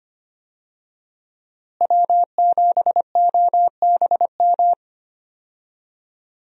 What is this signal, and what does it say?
Morse code, heard as a keyed tone, about 25 wpm, 700 Hz. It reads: W7OBM
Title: USA Call Signs - 5 Characters - Repeated 2x - 25wpm